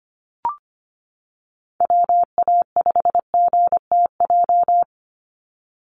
Morse code, heard as a keyed tone, about 25 wpm, 700 Hz. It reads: WA5GTJ
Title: USA Call Signs 25wpm